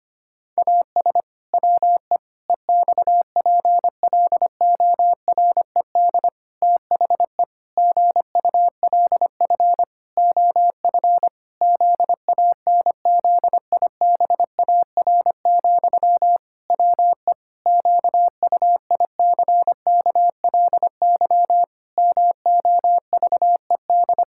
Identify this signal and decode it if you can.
Morse code, heard as a keyed tone, about 25 wpm, 700 Hz. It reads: AS WE EXPLORED THE GULF OF ZANZIBAR, WE QUICKLY MOVED
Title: Pangram Challenge v2 - 25wpm